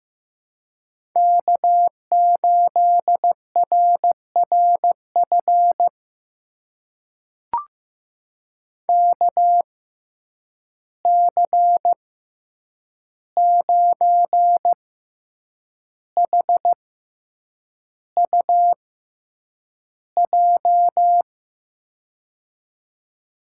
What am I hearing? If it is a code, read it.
Morse code, heard as a keyed tone, about 15 wpm, spaced as for 7 wpm, 700 Hz. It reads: K8RRF KC9HUJ
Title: USA Call Signs - 6x Character Spacing - 15wpm